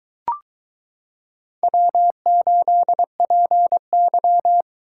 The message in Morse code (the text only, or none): W8PY